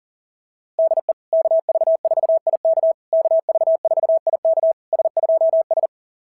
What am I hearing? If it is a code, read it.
Morse code, heard as a keyed tone, about 40 wpm, 660 Hz. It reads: DE KV4IK KV4IK S2S
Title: SOTA QSO Practice 40wpm